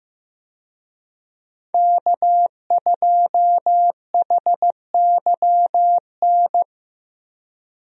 Morse code, heard as a keyed tone, about 15 wpm, 700 Hz. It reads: K2HYN
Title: USA Call Signs - 5 Characters - 15wpm